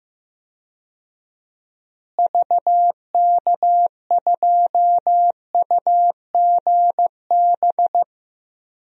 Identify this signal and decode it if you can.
Morse code, heard as a keyed tone, about 15 wpm, 700 Hz. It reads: VK2UGB
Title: DXCC-100 Call Signs - 6 Characters - Repeated 3x - 15wpm